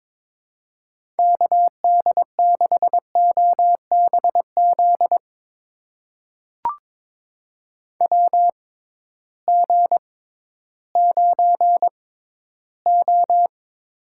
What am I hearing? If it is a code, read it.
Morse code, heard as a keyed tone, about 22 wpm, spaced as for 13 wpm, 700 Hz. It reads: KD6OBZ WG9O
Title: USA Call Signs - 6x Character Spacing - 22wpm